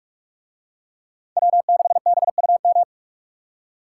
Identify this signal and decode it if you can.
Morse code, heard as a keyed tone, about 44 wpm, 700 Hz. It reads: W6BUM